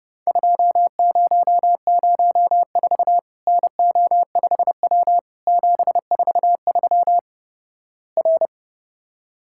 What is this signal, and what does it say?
Morse code, heard as a keyed tone, about 32 wpm, 700 Hz. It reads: 2004 NO5W 743 R
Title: Worked All Europe DX Contest - QTC Exchange - 30wpm